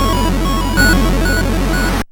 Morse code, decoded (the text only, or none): none